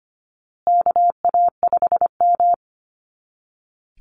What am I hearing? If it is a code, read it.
Morse code, heard as a keyed tone, about 25 wpm, 700 Hz. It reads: KA5M
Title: Call Signs Repeated 2x - 25wpm